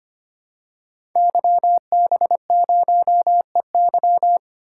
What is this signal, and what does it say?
Morse code, heard as a keyed tone, about 25 wpm, 700 Hz. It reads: YB0EY
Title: Code Groups - 5 Letter:Numbers - 25wpm